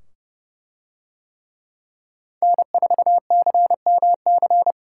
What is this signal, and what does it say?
Morse code, heard as a keyed tone, about 27 wpm, 700 Hz. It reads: N4CMC